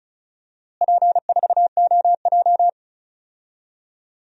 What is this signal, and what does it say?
Morse code, heard as a keyed tone, about 34 wpm, 700 Hz. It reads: P4OJ
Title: DXCC-100 Call Signs - 4 Characters - Repeated 3x - 35wpm